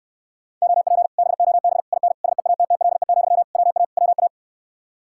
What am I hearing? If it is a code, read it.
Morse code, heard as a keyed tone, about 69 wpm, 700 Hz. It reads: OP BOB IN SEATTLE, WA WA